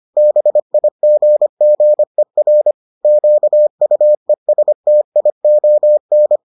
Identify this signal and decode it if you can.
Morse code, heard as a keyed tone, about 25 wpm, 600 Hz. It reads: BIGGER QUESTION